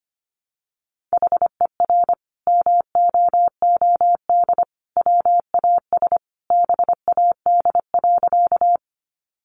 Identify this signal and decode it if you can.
Morse code, heard as a keyed tone, about 25 wpm, 700 Hz. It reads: HER MOOD WAS BAD.
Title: Sentences from Top 1100 Words 25wpm